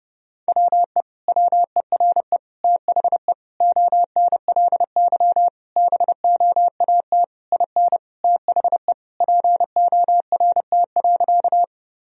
Morse code, heard as a keyed tone, about 30 wpm, 700 Hz. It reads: WE WERE THE ONLY BOAT IN THE PORT.
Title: Sentences from Top 1000 Words - Review Part 1 - 30wpm